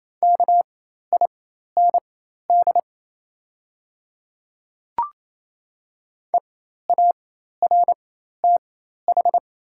Morse code, heard as a keyed tone, about 27 wpm, spaced as for 13 wpm, 700 Hz. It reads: KIND EARTH
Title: 4x Character Spacing - Top 500 Words - 28wpm